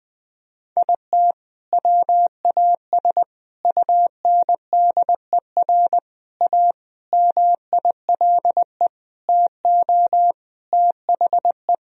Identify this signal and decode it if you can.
Morse code, heard as a keyed tone, about 20 wpm, 700 Hz. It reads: IT WAS UNDER A MILE TO THE
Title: Sentences from Top 1000 Words - Review Part 1 - 20wpm